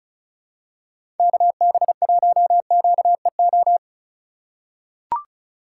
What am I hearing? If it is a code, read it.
Morse code, heard as a keyed tone, about 36 wpm, 700 Hz. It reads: KB1QEO
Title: USA Call Signs - 4x Character Spacing - 35wpm